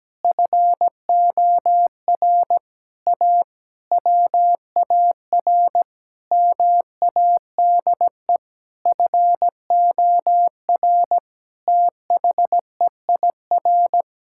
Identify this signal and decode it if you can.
Morse code, heard as a keyed tone, about 17 wpm, 700 Hz. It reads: FOR A WAR MADE FOR THEIR
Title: Sentences from Top 1000 Words - Review Part 6 - 17wpm